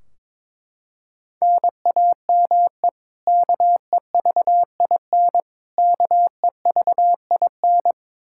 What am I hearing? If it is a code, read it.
Morse code, heard as a keyed tone, about 22 wpm, 700 Hz. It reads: NAME KEVIN KEVIN
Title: Practice QSO - Exchange 1 - Beginning - 22wpm